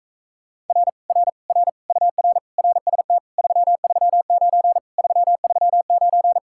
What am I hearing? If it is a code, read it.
Morse code, heard as a keyed tone, about 45 wpm, 700 Hz. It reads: R R R UR RST 339 339